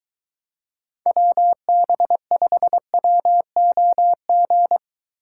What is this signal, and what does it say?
Morse code, heard as a keyed tone, about 23 wpm, 700 Hz. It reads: WB5WOG